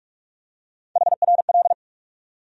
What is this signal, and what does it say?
Morse code, heard as a keyed tone, about 46 wpm, 700 Hz. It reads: HRL
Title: USA Call Sign Postfixes - 3 Letters 45wpm